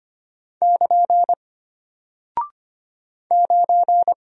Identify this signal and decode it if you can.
Morse code, heard as a keyed tone, about 25 wpm, 700 Hz. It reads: <KN> 9
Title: Single Character (Letter:Number:Punctuation:Prosign) Alt 25wpm